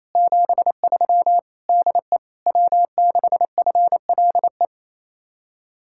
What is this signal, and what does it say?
Morse code, heard as a keyed tone, about 33 wpm, 700 Hz. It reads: 73 DE W6FLE
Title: Speed-Racing Common QSO Phrases - 28wpm